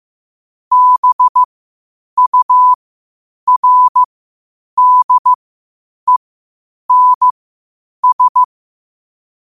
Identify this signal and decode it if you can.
Morse code, heard as a keyed tone, about 15 wpm, spaced as for 8 wpm, 1000 Hz. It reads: BURDENS
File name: qotdfiles / 8wpm